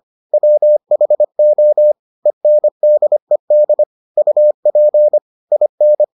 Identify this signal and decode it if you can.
Morse code, heard as a keyed tone, about 25 wpm, 600 Hz. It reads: WHO ENDED UP IN